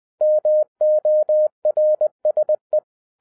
Morse code, heard as a keyed tone, about 20 wpm, 610 Hz. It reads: MORSE
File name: 2011-07-mp3 / n08 A LOT HAS CHANGED 20wpm